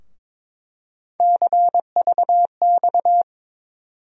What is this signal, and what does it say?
Morse code, heard as a keyed tone, about 22 wpm, 700 Hz. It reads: CVX
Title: USA Call Sign Postfixes - 3 Letters 22wpm